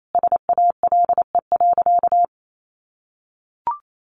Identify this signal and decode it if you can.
Morse code, heard as a keyed tone, about 28 wpm, 700 Hz. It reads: SALE.
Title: Sentences from Top 1400 Words 28wpm